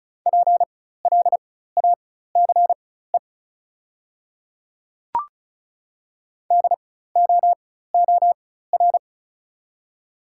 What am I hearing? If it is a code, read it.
Morse code, heard as a keyed tone, about 36 wpm, spaced as for 16 wpm, 690 Hz. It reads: PLACE DOOR
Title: 4x Character Spacing - Top 300 Words - 35wpm